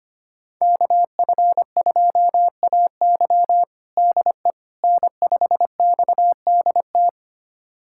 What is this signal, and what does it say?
Morse code, heard as a keyed tone, about 25 wpm, 700 Hz. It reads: KF2AY DE N5XDT
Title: USA Callsign DE Callsign - 5 Characters - 25wpm